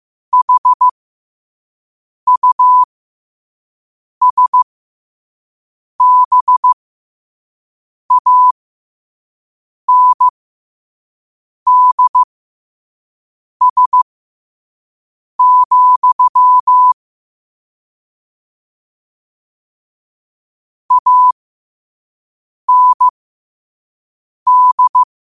Morse code, heard as a keyed tone, about 15 wpm, spaced as for 5 wpm, 1000 Hz. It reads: HUSBANDS, AND
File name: 5wpm